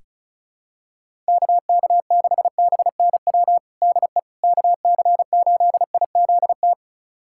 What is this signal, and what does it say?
Morse code, heard as a keyed tone, about 36 wpm, 700 Hz. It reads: KK6BNW DE KC8IZT
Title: USA Callsign DE Callsign 35wpm